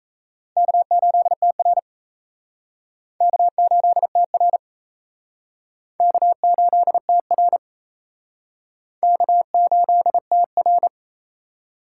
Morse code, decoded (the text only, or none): K8TR K8TR K8TR K8TR